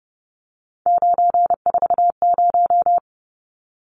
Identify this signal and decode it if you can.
Morse code, heard as a keyed tone, about 30 wpm, 700 Hz. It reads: 940
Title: Numbers - 3-Digit 30wpm